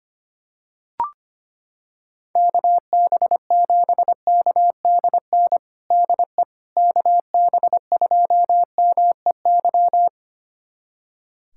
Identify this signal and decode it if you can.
Morse code, heard as a keyed tone, about 25 wpm, 700 Hz. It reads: KB7KDN DE KB2MEY
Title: USA Callsign DE Callsign - 6 Characters - 25wpm